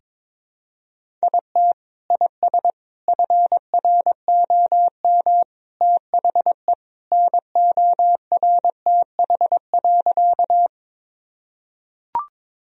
Morse code, heard as a keyed tone, about 22 wpm, 700 Hz. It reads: IT IS FROM THE NORTH.
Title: Sentences from Top 1000 Words - Review Part 1 - 22wpm